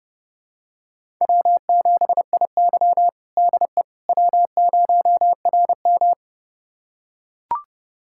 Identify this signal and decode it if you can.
Morse code, heard as a keyed tone, about 30 wpm, 700 Hz. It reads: W7IY DE W0RM
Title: USA Callsign DE Callsign - 4 Characters - 30wpm